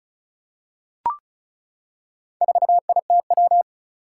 Morse code, heard as a keyed tone, about 36 wpm, 700 Hz. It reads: 4ITW
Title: Four Challenging Letter-Number Combinations 35wpm